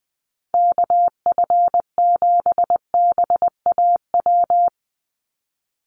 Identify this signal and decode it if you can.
Morse code, heard as a keyed tone, about 20 wpm, 700 Hz. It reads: KF7BAW
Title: USA Call Signs 20wpm